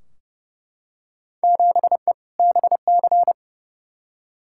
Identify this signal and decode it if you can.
Morse code, heard as a keyed tone, about 32 wpm, 700 Hz. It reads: 7E BC
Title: Field Day Exchanges Encore - 30wpm